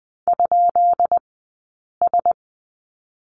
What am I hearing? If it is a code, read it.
Morse code, heard as a keyed tone, about 20 wpm, 700 Hz. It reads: ? S
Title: 183 Lesson 30d - Letter b - Two Characters 20wpm